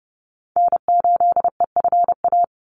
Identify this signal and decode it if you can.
Morse code, heard as a keyed tone, about 30 wpm, 700 Hz. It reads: N8EFA
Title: USA Call Signs 30wpm